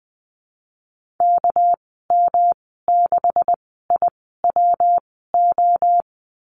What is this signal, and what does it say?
Morse code, heard as a keyed tone, about 20 wpm, spaced as for 15 wpm, 700 Hz. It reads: KM6IWO
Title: 231 Lesson 37g - Letter x - Call Signs 20wpm